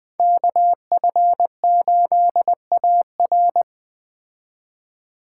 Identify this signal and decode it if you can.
Morse code, heard as a keyed tone, about 20 wpm, 700 Hz. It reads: KF8AR
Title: Speed Racing - USA Call Signs 17wpm